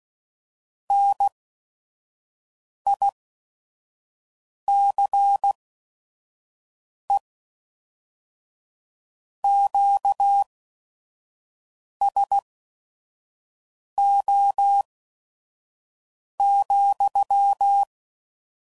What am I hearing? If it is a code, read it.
Morse code, heard as a keyed tone, about 16 wpm, spaced as for 5 wpm, 800 Hz. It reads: NICE QSO,
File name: CW Code Groups @5WPM / QSOs / kc7haa